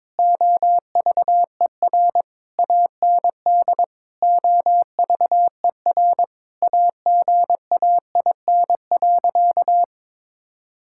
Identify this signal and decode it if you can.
Morse code, heard as a keyed tone, about 22 wpm, 700 Hz. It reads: OVER AND OVER AGAIN.
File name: practice / 22wpm / Sentences with 4-Word Phrases - 22wpm